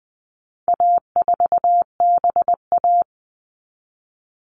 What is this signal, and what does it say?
Morse code, heard as a keyed tone, about 20 wpm, 700 Hz. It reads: A4BA
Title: All Cut Numbers - 4-Digit 20wpm